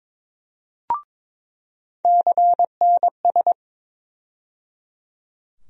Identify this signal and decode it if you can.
Morse code, heard as a keyed tone, about 22 wpm, 700 Hz. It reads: CNS
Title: USA Call Sign Postfixes - 3 Letters 22wpm